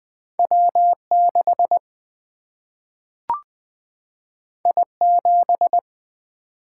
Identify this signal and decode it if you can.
Morse code, heard as a keyed tone, about 20 wpm, 700 Hz. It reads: W 6 I 7
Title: Single-Letter Single-Number 20wpm